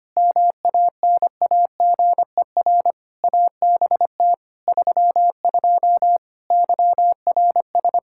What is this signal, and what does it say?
Morse code, heard as a keyed tone, about 25 wpm, 700 Hz. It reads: MANAGER ABT 32 YRS
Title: Practice QSO - Exchange 3 - More Info - 25wpm